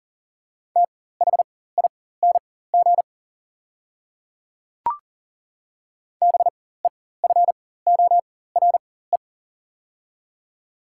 Morse code, heard as a keyed tone, about 40 wpm, spaced as for 19 wpm, 700 Hz. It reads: THING BEFORE
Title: 4x Character Spacing - Top 200 Words - 40wpm